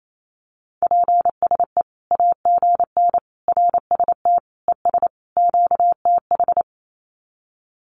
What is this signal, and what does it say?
Morse code, heard as a keyed tone, about 28 wpm, 700 Hz. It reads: PSE AGN RST ES QTH
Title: Common QSO Phrases 28wpm